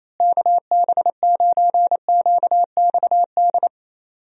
Morse code, heard as a keyed tone, about 28 wpm, 700 Hz. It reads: KB9QXD
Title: USA Call Signs - 6 Characters - Repeated 3x - 28wpm